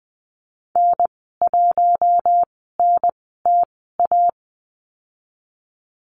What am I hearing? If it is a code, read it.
Morse code, heard as a keyed tone, about 20 wpm, spaced as for 15 wpm, 700 Hz. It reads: N1NTA
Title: 048 Lesson 9g - Number 4 - Call Signs 20wpm